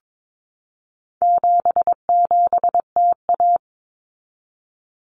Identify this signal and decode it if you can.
Morse code, heard as a keyed tone, about 22 wpm, 700 Hz. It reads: 77TA
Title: Common Cut Numbers - 4-Digit 22wpm